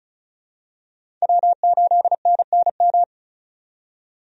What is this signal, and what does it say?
Morse code, heard as a keyed tone, about 34 wpm, 700 Hz. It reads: W8NNM
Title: USA Call Signs - 5 Characters - 35wpm